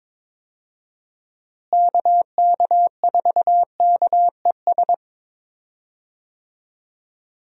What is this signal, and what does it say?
Morse code, heard as a keyed tone, about 22 wpm, 700 Hz. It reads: KK4KES
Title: USA Call Signs - 6 Characters - Repeated 2x - 22wpm